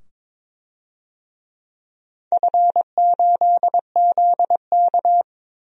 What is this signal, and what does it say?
Morse code, heard as a keyed tone, about 22 wpm, 700 Hz. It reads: F8ZK
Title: DXCC-100 Call Signs - 4 Characters - 22wpm